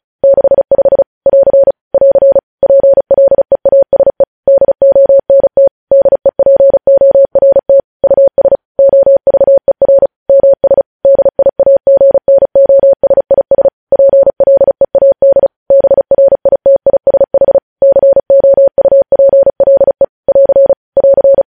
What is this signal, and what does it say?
Morse code, heard as a keyed tone, about 36 wpm, 550 Hz. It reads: 65 <AR> <AR> PLEASE DONT DEPORT US OVER MS DIAGNOSIS PLEAD BRITISH COUPLE <AR> <AR>
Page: News Headlines in Morse Code at 35 WPM